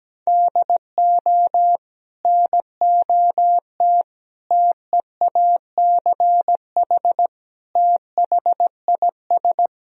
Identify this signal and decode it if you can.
Morse code, heard as a keyed tone, about 17 wpm, 700 Hz. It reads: DO NOT TEACH THIS
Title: Sentences from Top 1000 Words - Review Part 2 - 17wpm